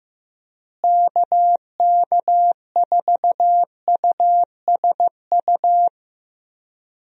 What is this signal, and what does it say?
Morse code, heard as a keyed tone, about 15 wpm, 700 Hz. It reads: KK4USU
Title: USA Call Signs - 6 Characters - Repeated 3x - 15wpm